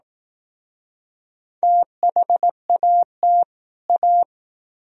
Morse code, heard as a keyed tone, about 19 wpm, 700 Hz. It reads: THAT A